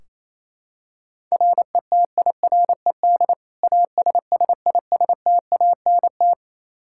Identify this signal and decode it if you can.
Morse code, heard as a keyed tone, about 28 wpm, 700 Hz. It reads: RETIRED ASSISTANT
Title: Practice QSO - Exchange 3 - More Info - 28wpm